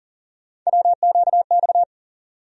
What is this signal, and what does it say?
Morse code, heard as a keyed tone, about 40 wpm, 700 Hz. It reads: WQX
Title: USA Call Sign Postfixes - 3 Letters 40wpm